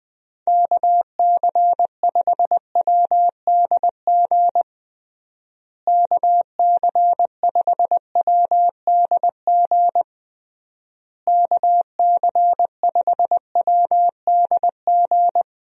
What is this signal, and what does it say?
Morse code, heard as a keyed tone, about 20 wpm, 700 Hz. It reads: KC5WDG KC5WDG KC5WDG
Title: USA Call Signs - 6 Characters - Repeated 3x - 20wpm